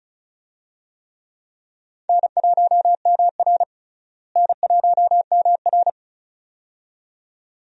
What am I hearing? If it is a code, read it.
Morse code, heard as a keyed tone, about 34 wpm, 700 Hz. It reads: N1MR N1MR
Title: USA Call Signs - 4 Characters - Repeated 2x - 35wpm